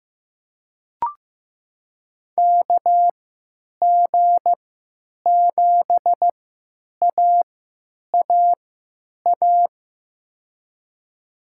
Morse code, heard as a keyed tone, about 15 wpm, spaced as for 8 wpm, 700 Hz. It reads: KG7AAA
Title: USA Call Signs - 3x Character Spacing - 15wpm